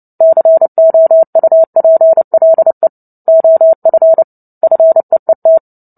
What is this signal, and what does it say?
Morse code, heard as a keyed tone, about 30 wpm, 650 Hz. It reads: COUPLE OF FEET
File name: Book Two / 03-Chapter 3 - 29 WPM